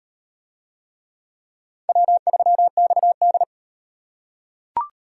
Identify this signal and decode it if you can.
Morse code, heard as a keyed tone, about 38 wpm, 700 Hz. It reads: W3XD